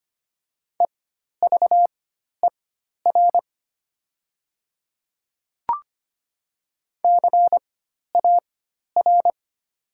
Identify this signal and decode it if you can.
Morse code, heard as a keyed tone, about 25 wpm, spaced as for 12 wpm, 700 Hz. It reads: EVER CAR
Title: 4x Character Spacing - Top 500 Words - 25wpm